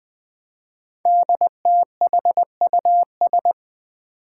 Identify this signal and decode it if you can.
Morse code, heard as a keyed tone, about 20 wpm, 700 Hz. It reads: DTHUS
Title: Five Challenging Letter-Number Combinations 20wpm